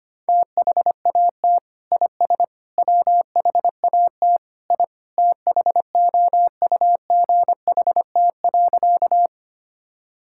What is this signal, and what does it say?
Morse code, heard as a keyed tone, about 25 wpm, 700 Hz. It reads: THAT IS WHAT I THOUGHT.
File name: practice / 25wpm / Sentences with 2-Word Phrases - 25wpm